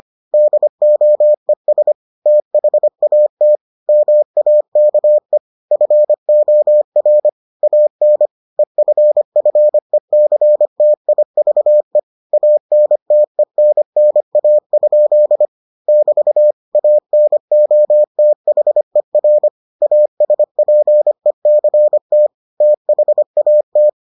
Morse code, heard as a keyed tone, about 25 wpm, 600 Hz. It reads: DOES THAT MAKE FOR AN EFFECTIVE ANTENNA? <BT> ANOTHER ASPECT THAT